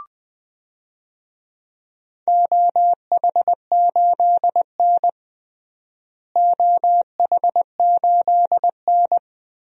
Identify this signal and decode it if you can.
Morse code, heard as a keyed tone, about 20 wpm, 700 Hz. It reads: OH8N OH8N
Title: DXCC-100 Call Signs - 4 Characters - Repeated 3x - 20wpm